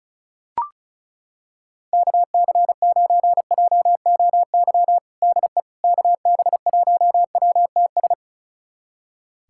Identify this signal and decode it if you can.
Morse code, heard as a keyed tone, about 34 wpm, 690 Hz. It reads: KC9JOY DE KB1WTS
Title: USA Callsign DE Callsign - 6 Characters - 35wpm